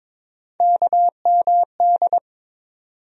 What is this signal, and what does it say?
Morse code, heard as a keyed tone, about 22 wpm, 700 Hz. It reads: KMD
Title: USA Call Sign Postfixes - 3 Letters 22wpm